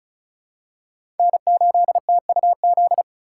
Morse code, heard as a keyed tone, about 36 wpm, 700 Hz. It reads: N8TUZ
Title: USA Call Signs - 5 Characters - 3x Character Spacing - 35wpm